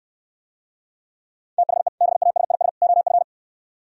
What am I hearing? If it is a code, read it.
Morse code, heard as a keyed tone, about 66 wpm, 700 Hz. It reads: THE BASIS OF